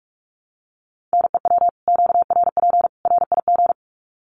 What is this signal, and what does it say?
Morse code, heard as a keyed tone, about 44 wpm, 700 Hz. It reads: NEW QRP RIG
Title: Common QSO Phrases 45wpm